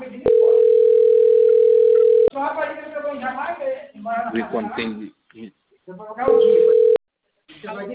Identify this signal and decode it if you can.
Ringback tone (North America), 440 + 480 Hz, with speech.
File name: out- - - - - .14424